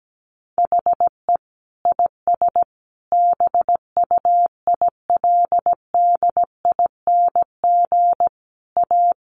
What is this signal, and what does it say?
Morse code, heard as a keyed tone, about 17 wpm, 700 Hz. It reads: HE IS BUILDING A
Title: Sentences from Top 700 Words 17wpm